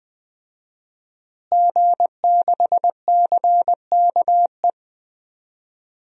Morse code, heard as a keyed tone, about 20 wpm, 700 Hz. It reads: G6CKE
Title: DXCC-100 Call Signs - 5 Characters - Repeated 2x - 20wpm